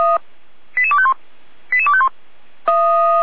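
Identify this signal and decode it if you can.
Five-tone selective call (ZVEI), heard as quick runs of five tones, tones of about 70 ms. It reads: 88251 88251
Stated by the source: ZVEI five-tone selective call received on an SDR, fire brigade alerting.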